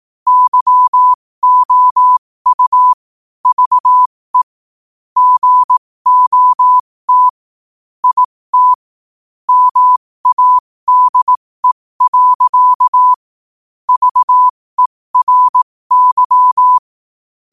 Morse code, heard as a keyed tone, about 18 wpm, spaced as for 15 wpm, 1000 Hz. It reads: YOU VE GOT IT MADE. VERY
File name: qotdfiles / 15wpm